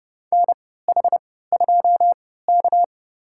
Morse code, heard as a keyed tone, about 32 wpm, spaced as for 17 wpm, 700 Hz. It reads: NH2K
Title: USA Call Signs - 4 Characters - 3x Character Spacing - 30wpm